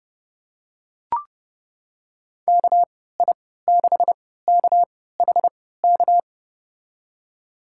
Morse code, heard as a keyed tone, about 30 wpm, spaced as for 17 wpm, 700 Hz. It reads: KI6KHK
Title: USA Call Signs - 6 Characters - 3x Character Spacing - 30wpm